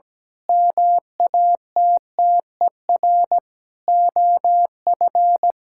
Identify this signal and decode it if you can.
Morse code, heard as a keyed tone, about 17 wpm, 700 Hz. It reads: MATTER OF
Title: Sentences from Top 1000 Words - Review Part 5 - 17wpm